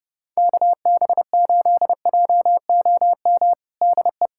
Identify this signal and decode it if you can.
Morse code, heard as a keyed tone, about 32 wpm, 700 Hz. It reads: KB8JOM DE
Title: USA Callsign DE Callsign - 6 Characters - 30wpm